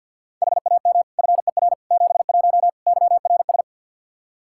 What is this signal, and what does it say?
Morse code, heard as a keyed tone, about 46 wpm, 700 Hz. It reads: HAM FER 71 YRS
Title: K1USN SST QSO Practice 50wpm